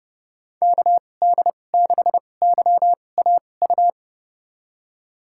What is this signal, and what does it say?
Morse code, heard as a keyed tone, about 30 wpm, spaced as for 22 wpm, 700 Hz. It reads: KD6YAU
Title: USA Call Signs - 2x Character Spacing - 30wpm